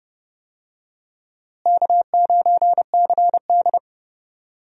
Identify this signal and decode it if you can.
Morse code, heard as a keyed tone, about 30 wpm, 700 Hz. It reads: K9CD
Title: USA Call Signs - 4 Characters - Repeated 2x - 30wpm